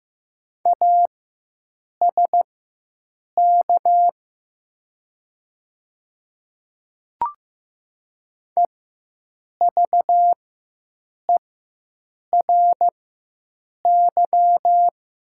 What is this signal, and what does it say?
Morse code, heard as a keyed tone, about 15 wpm, spaced as for 7 wpm, 700 Hz. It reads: ASK EVERY
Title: 4x Character Spacing - Top 300 Words - 15wpm